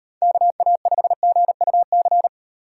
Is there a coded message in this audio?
Yes: Morse code from 0.2 to 2.3 s, heard as a keyed tone, about 38 wpm, 700 Hz. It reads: KA5GUC